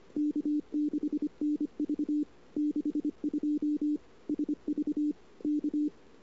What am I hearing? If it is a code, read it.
Morse code, heard as a keyed tone, about 25 wpm, 300 Hz. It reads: K6NV 62 SV K
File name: CW / 2008-ss-cw-k5tr-op-n5rz -22:00